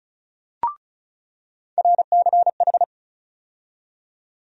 Morse code, heard as a keyed tone, about 36 wpm, 700 Hz. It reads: RCH
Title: USA Call Sign Postfixes - 3 Letters 35wpm